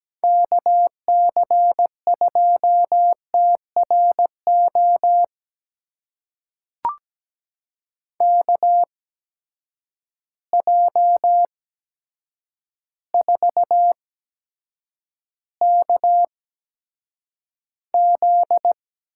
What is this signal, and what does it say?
Morse code, heard as a keyed tone, about 17 wpm, spaced as for 8 wpm, 700 Hz. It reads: KC2TRO KJ4KZ
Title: USA Call Signs - 6 Characters - 8x Character Spacing - 17wpm